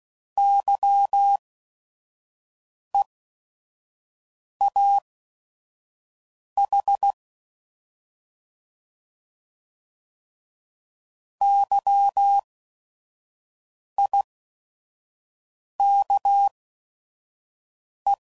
Morse code, heard as a keyed tone, about 16 wpm, spaced as for 5 wpm, 800 Hz. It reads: YEAH YIKE